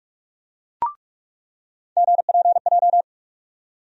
Morse code, heard as a keyed tone, about 46 wpm, 700 Hz. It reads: GPJ
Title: USA Call Sign Postfixes - 3 Letters 45wpm